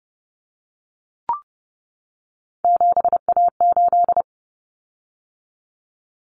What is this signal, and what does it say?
Morse code, heard as a keyed tone, about 30 wpm, 700 Hz. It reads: 7A8
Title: Common Cut Numbers - 3-Digit 30wpm